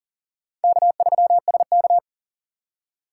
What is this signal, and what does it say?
Morse code, heard as a keyed tone, about 40 wpm, 700 Hz. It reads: K3SK
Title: USA Call Signs - 4 Characters - 6x Character Spacing - 40wpm